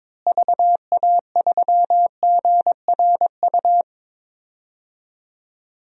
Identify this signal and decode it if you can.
Morse code, heard as a keyed tone, about 22 wpm, 700 Hz. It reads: VA3GRU
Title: DXCC-100 Call Signs - 6 Characters - Repeated 2x - 22wpm